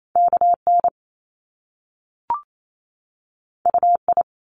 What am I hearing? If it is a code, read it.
Morse code, heard as a keyed tone, about 28 wpm, 700 Hz. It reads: KN UI
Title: Top Two Letter Combinations in Words 28wpm